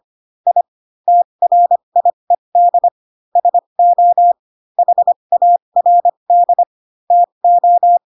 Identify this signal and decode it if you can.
Morse code, heard as a keyed tone, about 25 wpm, spaced as for 22 wpm, 700 Hz. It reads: I TRIED SO HARD TO
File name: Pinocchio / Pinocchio ch3 18 25 ik1hge